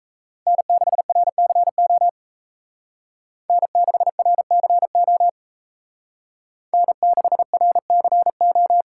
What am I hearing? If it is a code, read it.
Morse code, heard as a keyed tone, about 36 wpm, 700 Hz. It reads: N6RCO N6RCO N6RCO